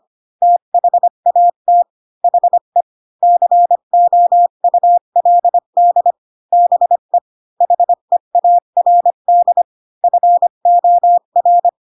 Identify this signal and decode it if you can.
Morse code, heard as a keyed tone, about 25 wpm, 700 Hz. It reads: THAT HE COULD BE HEARD FOR
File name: Pinocchio / Pinocchio ch7 20 25 ik1hge